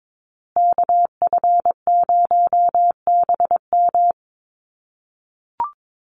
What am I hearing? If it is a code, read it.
Morse code, heard as a keyed tone, about 22 wpm, 700 Hz. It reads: KF0BM
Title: USA Call Signs Repeated 2x 22wpm